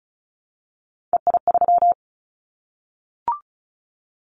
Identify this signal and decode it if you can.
Morse code, heard as a keyed tone, about 36 wpm, 700 Hz. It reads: EI3
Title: Callsign Prefixes for Top 50 DXCC - 35wpm